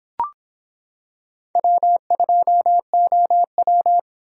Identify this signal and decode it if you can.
Morse code, heard as a keyed tone, about 25 wpm, 700 Hz. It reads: W2OW